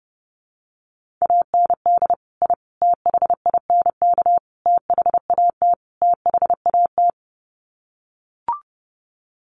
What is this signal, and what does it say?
Morse code, heard as a keyed tone, about 30 wpm, 700 Hz. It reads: AND I THINK THAT THAT
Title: Sets of 5 Words Encore 30wpm